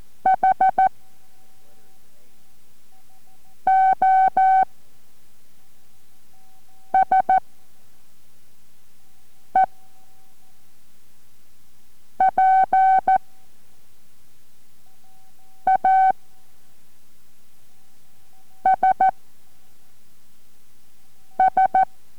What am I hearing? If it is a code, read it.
Morse code, heard as a keyed tone, about 14 wpm, 760 Hz. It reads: H O S E P A S S